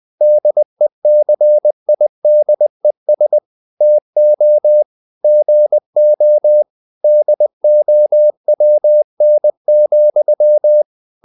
Morse code, heard as a keyed tone, about 20 wpm, 600 Hz. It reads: DECIDES TO GO DOWN,
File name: the old man and the sea / the old man and the sea part 1 20wpm